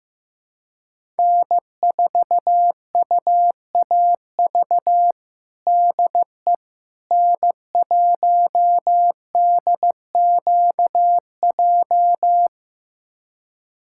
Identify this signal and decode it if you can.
Morse code, heard as a keyed tone, about 15 wpm, 700 Hz. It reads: N4UAV DE N1DQJ
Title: USA Callsign DE Callsign - 5 Characters - 15wpm